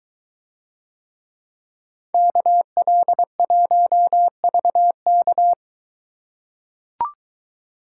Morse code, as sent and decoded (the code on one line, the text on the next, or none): -.- .-.. .---- ...- -.-
KL1VK